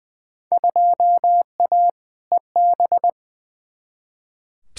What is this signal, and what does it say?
Morse code, heard as a keyed tone, about 20 wpm, 700 Hz. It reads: 2A EB
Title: Field Day Exchanges - 20wpm